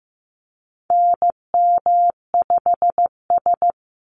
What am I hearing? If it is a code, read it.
Morse code, heard as a keyed tone, about 15 wpm, 700 Hz. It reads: NM5S
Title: NA SOTA Activator Callsigns 15wpm